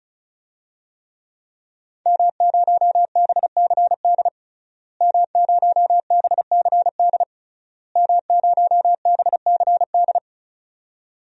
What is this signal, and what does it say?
Morse code, heard as a keyed tone, about 36 wpm, 700 Hz. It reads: M0BCD M0BCD M0BCD
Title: DXCC-100 Call Signs - 5 Characters - Repeated 3x - 35wpm